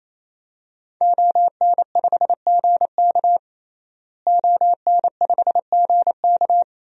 Morse code, heard as a keyed tone, about 28 wpm, 700 Hz. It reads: ON5GK ON5GK
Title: DXCC-100 Call Signs - 5 Characters - Repeated 3x - 28wpm